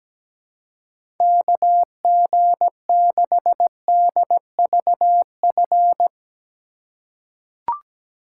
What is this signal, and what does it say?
Morse code, heard as a keyed tone, about 17 wpm, 700 Hz. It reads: KG6DVF
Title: USA Call Signs - 6 Characters - 3x Character Spacing - 17wpm